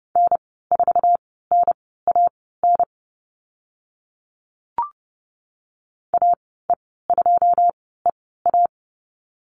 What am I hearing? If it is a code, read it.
Morse code, heard as a keyed tone, about 30 wpm, spaced as for 17 wpm, 700 Hz. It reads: N4NAN AE2EA
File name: Lesson 14g - Number 2 - Call Signs 30wpm